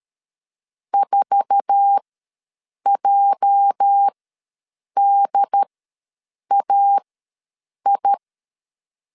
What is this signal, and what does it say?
Morse code, heard as a keyed tone, about 13 wpm, spaced as for 7 wpm, 800 Hz. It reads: 4JDAI
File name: LEZIONI / 8 YJ45 / 8 YJ4 ws 12wpm 3249